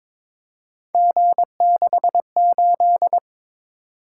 Morse code, heard as a keyed tone, about 22 wpm, 700 Hz. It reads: G68
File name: practice / 22wpm / Three Letter-Number Combinations - 22wpm